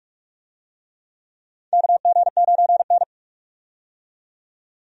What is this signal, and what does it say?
Morse code, heard as a keyed tone, about 46 wpm, 700 Hz. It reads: KG9N
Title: USA Call Signs - 4 Characters - 45wpm